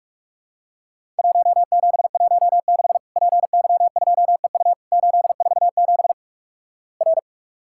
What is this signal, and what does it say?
Morse code, heard as a keyed tone, about 46 wpm, 700 Hz. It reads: 1716 PY2EU 847 R
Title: Worked All Europe DX Contest - QTC Exchange - 45wpm